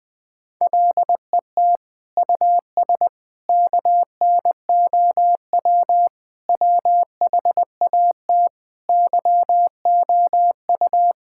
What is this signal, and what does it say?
Morse code, heard as a keyed tone, about 20 wpm, 700 Hz. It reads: LET US KNOW WHAT YOU
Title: Sentences with 9-Word Phrases - 20wpm